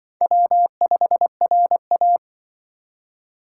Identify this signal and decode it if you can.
Morse code, heard as a keyed tone, about 25 wpm, 700 Hz. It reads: W5RA